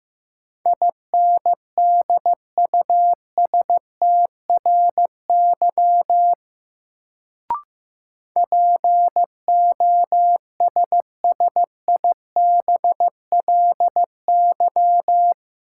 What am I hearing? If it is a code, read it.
Morse code, heard as a keyed tone, about 15 wpm, 700 Hz. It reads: INDUSTRY POSSIBLY
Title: Sentences from Top 1300 Words - New Word Drill Alt - 15wpm